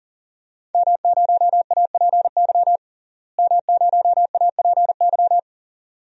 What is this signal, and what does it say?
Morse code, heard as a keyed tone, about 40 wpm, 700 Hz. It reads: M0APY M0APY
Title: DXCC-100 Call Signs - 5 Characters - Repeated 2x - 40wpm